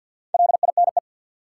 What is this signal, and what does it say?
Morse code, heard as a keyed tone, about 49 wpm, 700 Hz. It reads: LINE
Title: Sentences from Top 1200 Words - New Word Drill Alt - 50wpm